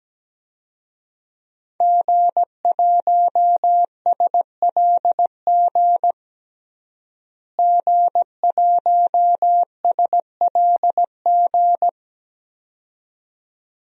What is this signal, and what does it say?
Morse code, heard as a keyed tone, about 17 wpm, 700 Hz. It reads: G1SLG G1SLG
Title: DXCC-100 Call Signs - 5 Characters - Repeated 2x - 17wpm